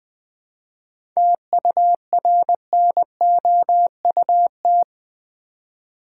Morse code, heard as a keyed tone, about 20 wpm, 700 Hz. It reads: TURNOUT
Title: 7 Letter Words - First 3 Letters Given - 20wpm